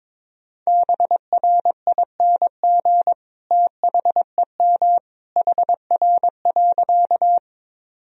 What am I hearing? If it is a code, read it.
Morse code, heard as a keyed tone, about 22 wpm, 700 Hz. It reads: BRING THEM HR.
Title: Sentences from Top 500 Words and Abbreviations 22wpm